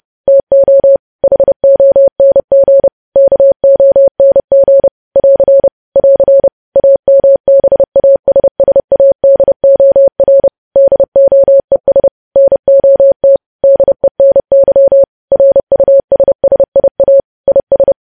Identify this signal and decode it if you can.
Morse code, heard as a keyed tone, about 30 wpm, 550 Hz. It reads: TO HONG KONG <AR> <AR> AMBASSADOR DOES NOT DENY RUSSIA IS
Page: news headlines in morse code at 30 wpm